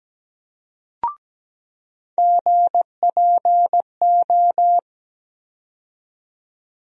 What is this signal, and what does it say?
Morse code, heard as a keyed tone, about 17 wpm, 700 Hz. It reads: GPO
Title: USA Call Sign Postfixes - 3 Letters 17wpm